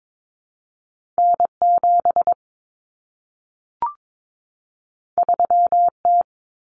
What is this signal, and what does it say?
Morse code, heard as a keyed tone, about 22 wpm, 700 Hz. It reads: N7 3T
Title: Common Cut Numbers - 2-Digit 22wpm